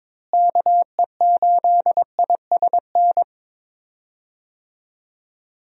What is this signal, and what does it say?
Morse code, heard as a keyed tone, about 22 wpm, 700 Hz. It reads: KE8ISN
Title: USA Call Signs - 6 Characters - Repeated 2x - 22wpm